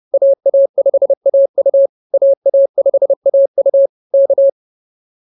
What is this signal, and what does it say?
Morse code, heard as a keyed tone, about 28 wpm, 560 Hz. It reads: AA5AU AA5AU K